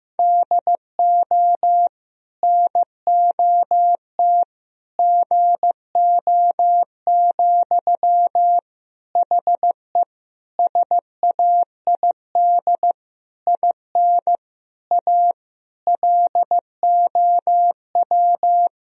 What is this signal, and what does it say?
Morse code, heard as a keyed tone, about 15 wpm, 700 Hz. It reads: DO NOT GO, HE SAID IN A LOW
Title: Sentences from Top 1000 Words - Review Part 1 - 15wpm